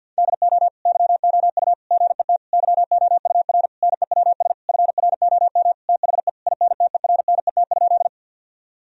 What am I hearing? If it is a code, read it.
Morse code, heard as a keyed tone, about 50 wpm, 700 Hz. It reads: DO YOU GET YOUR NEWS FROM THE INTERNET?
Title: Sentences from Top 1000 Words - Review Part 4 - 50wpm